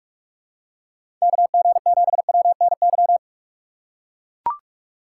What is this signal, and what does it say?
Morse code, heard as a keyed tone, about 42 wpm, 700 Hz. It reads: KG7WNY
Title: USA Call Signs - 6 Characters - 4x Character Spacing - 45wpm